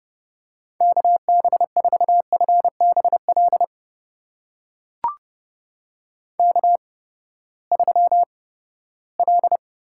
Morse code, heard as a keyed tone, about 30 wpm, spaced as for 17 wpm, 700 Hz. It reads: KB4FBL K3L
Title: USA Call Signs - 8x Character Spacing - 30wpm